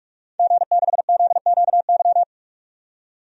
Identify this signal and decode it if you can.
Morse code, heard as a keyed tone, about 46 wpm, 700 Hz. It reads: G6ZQY
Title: DXCC-100 Call Signs - 5 Characters - 45wpm